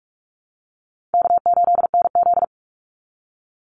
Morse code, heard as a keyed tone, about 44 wpm, 700 Hz. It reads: K8NZ
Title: Easy Call Signs Repeated 3x - 45wpm